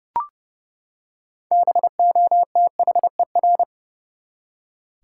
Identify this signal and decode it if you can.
Morse code, heard as a keyed tone, about 30 wpm, 700 Hz. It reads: BOTHER
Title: Six Letter Words - 30wpm